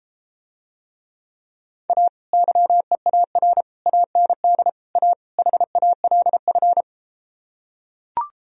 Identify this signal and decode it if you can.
Morse code, heard as a keyed tone, about 33 wpm, 700 Hz. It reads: A YEAR AND A HALF